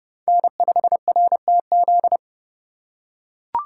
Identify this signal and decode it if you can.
Morse code, heard as a keyed tone, about 30 wpm, 700 Hz. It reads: N5RTZ
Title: USA Call Signs - 5 Characters - 2x Character Spacing - 30wpm